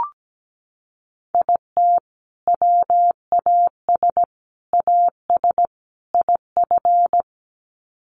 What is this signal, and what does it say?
Morse code, heard as a keyed tone, about 17 wpm, 700 Hz. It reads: IT WAS AS IF
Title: Sets of 4 Words 17wpm